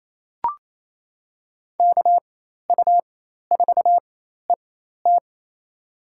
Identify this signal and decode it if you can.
Morse code, heard as a keyed tone, about 28 wpm, spaced as for 13 wpm, 700 Hz. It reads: KU4ET
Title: USA Call Signs - 5 Characters - 4x Character Spacing - 28wpm